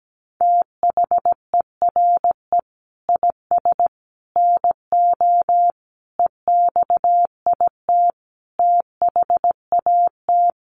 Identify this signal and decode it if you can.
Morse code, heard as a keyed tone, about 17 wpm, 700 Hz. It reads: THERE IS NO EXIT THAT
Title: Sentences from Top 1100 Words 17wpm